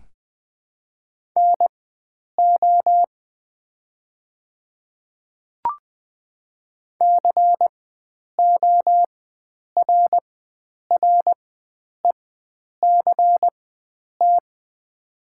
Morse code, heard as a keyed tone, about 20 wpm, spaced as for 9 wpm, 700 Hz. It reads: NO CORRECT
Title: 4x Character Spacing - Top 500 Words - 20wpm